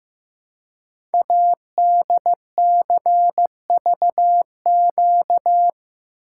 Morse code, heard as a keyed tone, about 15 wpm, 700 Hz. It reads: ADCVQ
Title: Code Groups - 5 Letters - 15wpm